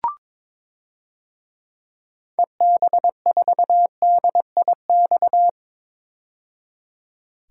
Morse code, heard as a keyed tone, about 22 wpm, 700 Hz. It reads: EB4DIX
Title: DXCC-100 Call Signs - 6 Characters - 22wpm